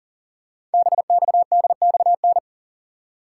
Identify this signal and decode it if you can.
Morse code, heard as a keyed tone, about 42 wpm, 690 Hz. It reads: BXDXN
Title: Code Groups - 5 Letter:Numbers - 40wpm